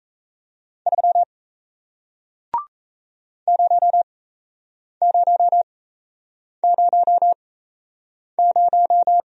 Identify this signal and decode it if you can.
Morse code, heard as a keyed tone, about 36 wpm, spaced as for 8 wpm, 700 Hz. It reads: 3 0000